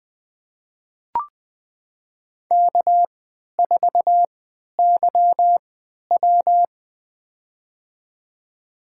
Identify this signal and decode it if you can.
Morse code, heard as a keyed tone, about 20 wpm, spaced as for 11 wpm, 700 Hz. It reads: K4YW
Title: USA Call Signs - 4 Characters - 3x Character Spacing - 20wpm